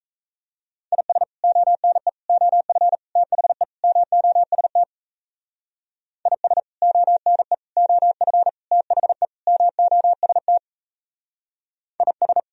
Morse code, heard as a keyed tone, about 36 wpm, 700 Hz. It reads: IS ONE OF THE MOST IS ONE OF THE MOST IS